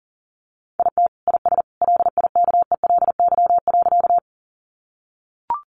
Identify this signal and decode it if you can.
Morse code, heard as a keyed tone, about 40 wpm, 700 Hz. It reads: IT IS LIKELY.
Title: Sentences from Top 600 Words 40wpm